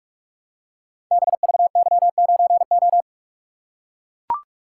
Morse code, heard as a keyed tone, about 44 wpm, 690 Hz. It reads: BVY9O
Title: Code Groups - 5 Letter:Numbers - 45wpm